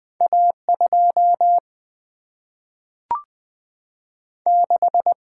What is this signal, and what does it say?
Morse code, heard as a keyed tone, about 20 wpm, 700 Hz. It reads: A 2 6
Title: Single-Letter Single-Number 20wpm